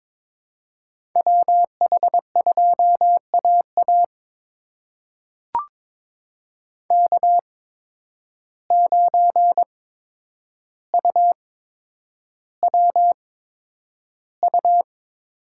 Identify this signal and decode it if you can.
Morse code, heard as a keyed tone, about 22 wpm, spaced as for 9 wpm, 700 Hz. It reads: WH2AA K9UWU
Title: USA Call Signs - 5 Characters - 8x Character Spacing - 22wpm